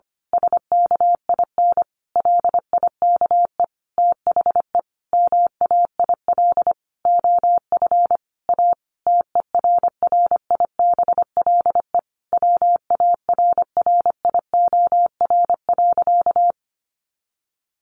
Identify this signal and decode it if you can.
Morse code, heard as a keyed tone, about 25 wpm, 700 Hz. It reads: SKIN LIKE THE MAIL OF A TERRIBLE WARRIOR.
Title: A Sound Of Thunder 25wpm